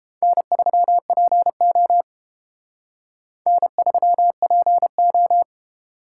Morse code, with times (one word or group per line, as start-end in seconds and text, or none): 0.2-2.0 N3PO
3.5-5.4 N3PO